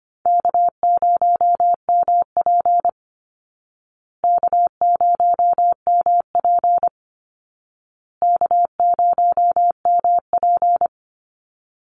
Morse code, heard as a keyed tone, about 24 wpm, 700 Hz. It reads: K0MP K0MP K0MP
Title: Easy Call Signs Repeated 3x - 25wpm